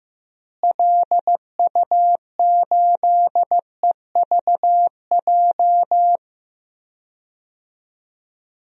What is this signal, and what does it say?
Morse code, heard as a keyed tone, about 16 wpm, 700 Hz. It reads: LU8EVJ
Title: DXCC-100 Call Signs - 6 Characters - Repeated 2x - 15wpm